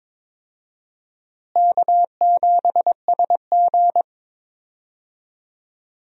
Morse code, heard as a keyed tone, about 22 wpm, 700 Hz. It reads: K7SG
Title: USA Call Signs - 4 Characters - Repeated 3x - 22wpm